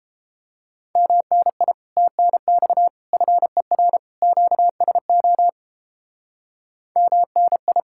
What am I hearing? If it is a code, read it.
Morse code, heard as a keyed tone, about 31 wpm, 700 Hz. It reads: MNI TNX FER QSO MNI